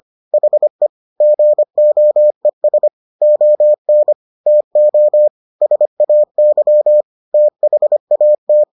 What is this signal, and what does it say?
Morse code, heard as a keyed tone, about 25 wpm, 600 Hz. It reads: HE GOES ON TO SAY THAT